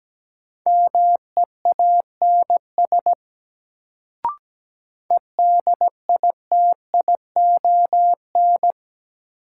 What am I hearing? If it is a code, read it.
Morse code, heard as a keyed tone, about 17 wpm, 700 Hz. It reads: MEANS EDITION
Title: Sets of 5 Words - New Word Drill - 17wpm